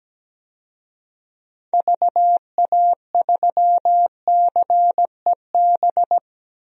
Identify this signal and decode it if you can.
Morse code, heard as a keyed tone, about 17 wpm, 700 Hz. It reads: VA3CEB
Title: DXCC-100 Call Signs - 6 Characters - Repeated 2x - 17wpm